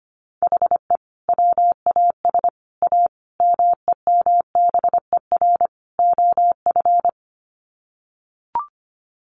Morse code, heard as a keyed tone, about 25 wpm, 700 Hz. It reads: HE WAS A MEMBER OF
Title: Sets of 5 Words Encore2 25wpm